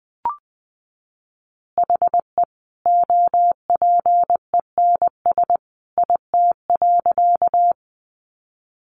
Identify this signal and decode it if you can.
Morse code, heard as a keyed tone, about 20 wpm, 700 Hz. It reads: HE OPENS IT.
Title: Additional Easy Phrases 20wpm